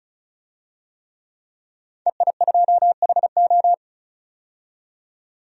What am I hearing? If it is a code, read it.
Morse code, heard as a keyed tone, about 36 wpm, 700 Hz. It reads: EI2HO
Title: DXCC-100 Call Signs - 5 Characters - 35wpm